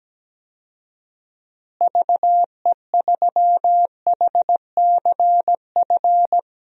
Morse code, heard as a keyed tone, about 17 wpm, 700 Hz. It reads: VE3HCF
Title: DXCC-100 Call Signs - 6 Characters - Repeated 3x - 17wpm